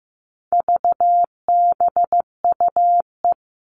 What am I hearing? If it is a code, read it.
Morse code, heard as a keyed tone, about 15 wpm, 700 Hz. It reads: VBUE
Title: All Cut Numbers - 4-Digit 15wpm